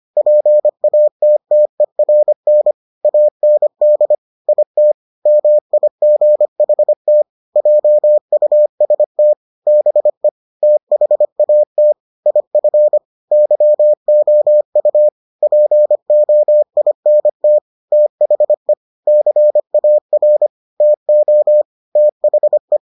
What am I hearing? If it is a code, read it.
Morse code, heard as a keyed tone, about 25 wpm, 600 Hz. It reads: PATTERN AND IT MIGHT JUST BE THAT IF YOU POINT THE CAR TO THE